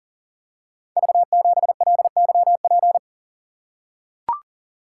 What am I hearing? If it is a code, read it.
Morse code, heard as a keyed tone, about 38 wpm, 700 Hz. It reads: V7LYP
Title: Code Groups - 5 Letter:Numbers - 40wpm